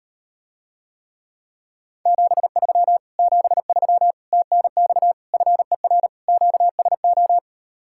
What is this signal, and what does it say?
Morse code, heard as a keyed tone, about 38 wpm, 700 Hz. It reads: 73 73 TNX FER QSO